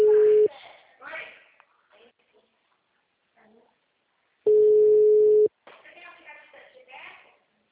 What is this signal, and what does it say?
Ringback tone (Europe, 425 Hz), with speech.